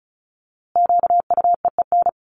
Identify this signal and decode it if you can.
Morse code, heard as a keyed tone, about 34 wpm, 700 Hz. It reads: QUEEN
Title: A Sound Of Thunder 35wpm